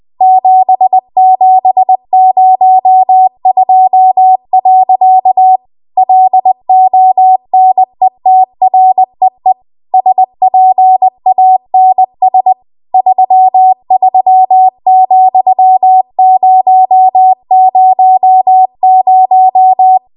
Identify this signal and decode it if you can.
Morse code, heard as a keyed tone, about 20 wpm, 750 Hz. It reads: 7702. LONETREE SPANS 33,000
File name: Archive / 20 WPM / 20WPM